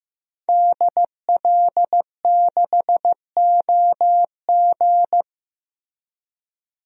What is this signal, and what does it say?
Morse code, heard as a keyed tone, about 15 wpm, 700 Hz. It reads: DL6OG
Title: DXCC-100 Call Signs - 5 Characters - Repeated 2x - 15wpm